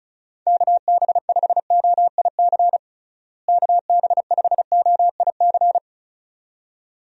Morse code, heard as a keyed tone, about 36 wpm, 700 Hz. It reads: KB5OIC KB5OIC
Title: USA Call Signs - 6 Characters - Repeated 2x - 35wpm